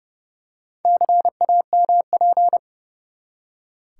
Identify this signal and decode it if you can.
Morse code, heard as a keyed tone, about 30 wpm, 700 Hz. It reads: CAMP
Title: Four Letter Words - 30wpm